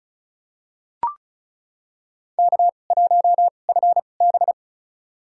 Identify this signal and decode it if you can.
Morse code, heard as a keyed tone, about 36 wpm, spaced as for 25 wpm, 700 Hz. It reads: K1FB
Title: USA Call Signs - 4 Characters - 2x Character Spacing - 35wpm